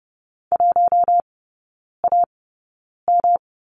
Morse code, heard as a keyed tone, about 30 wpm, spaced as for 9 wpm, 700 Hz. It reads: 1AM
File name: Lesson 18e - Letter m - Three Characters 30wpm